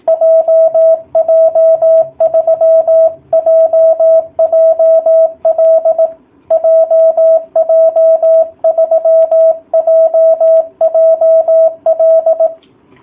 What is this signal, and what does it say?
Morse code, heard as a keyed tone, about 18 wpm, 660 Hz. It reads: JJ3JJL JJ3JJL